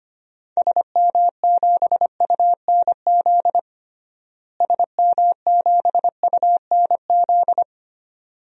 Morse code, heard as a keyed tone, about 25 wpm, 700 Hz. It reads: SM7UNZ SM7UNZ
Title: DXCC-100 Call Signs - 6 Characters - Repeated 2x - 25wpm